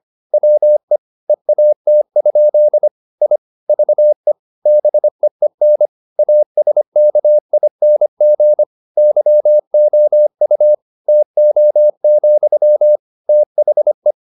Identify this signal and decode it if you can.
Morse code, heard as a keyed tone, about 25 wpm, 600 Hz. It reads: WE EAT? I VE BEEN ASKING YOU TO, THE
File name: the old man and the sea part 1 25wpm